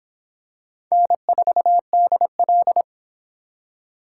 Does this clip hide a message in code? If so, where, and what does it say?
Yes: Morse code from 0.9 to 2.8 s, heard as a keyed tone, about 25 wpm, 700 Hz. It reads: N4DL